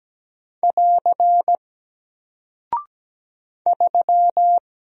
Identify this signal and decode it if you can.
Morse code, heard as a keyed tone, about 17 wpm, 700 Hz. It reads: <AR> 3
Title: Single Character (Letter:Number:Punctuation:Prosign) Alt 17wpm